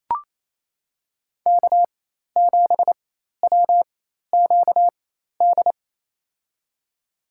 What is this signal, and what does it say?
Morse code, heard as a keyed tone, about 28 wpm, spaced as for 13 wpm, 690 Hz. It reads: K7WQD
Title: USA Call Signs - 5 Characters - 4x Character Spacing - 28wpm